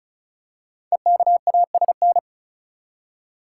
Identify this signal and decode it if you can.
Morse code, heard as a keyed tone, about 36 wpm, 700 Hz. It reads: EKASN
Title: Code Groups - 5 Letters - 35wpm